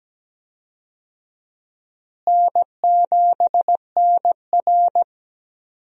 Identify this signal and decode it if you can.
Morse code, heard as a keyed tone, about 17 wpm, 700 Hz. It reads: N7NR
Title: USA Call Signs - 4 Characters - Repeated 2x - 17wpm